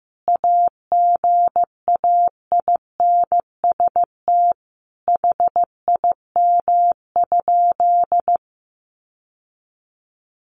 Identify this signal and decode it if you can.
Morse code, heard as a keyed tone, about 15 wpm, 700 Hz. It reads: AGAINST HIM?
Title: Sentences from Top 1000 Words 15wpm